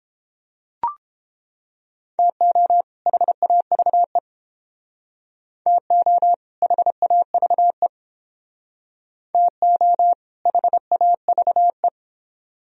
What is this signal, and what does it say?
Morse code, heard as a keyed tone, about 30 wpm, 700 Hz. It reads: TO HAVE TO HAVE TO HAVE